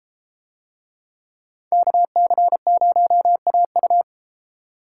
Morse code, heard as a keyed tone, about 30 wpm, 700 Hz. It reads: KC0AU